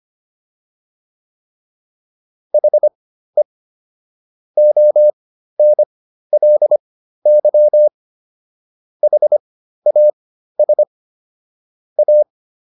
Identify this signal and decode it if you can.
Morse code, heard as a keyed tone, about 25 wpm, spaced as for 13 wpm, 600 Hz. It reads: HE ONLY HAS A